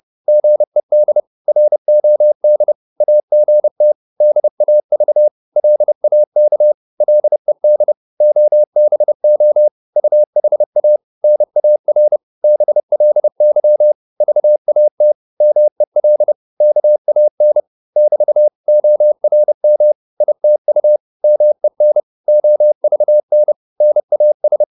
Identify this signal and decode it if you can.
Morse code, heard as a keyed tone, about 30 wpm, 600 Hz. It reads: GED ROD AGT DAV LAK LED OBO UHA NAR BLY VAT MEL KAN <BT> ORM ITU MEN OVN NAS
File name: Korte ord DK 30wpm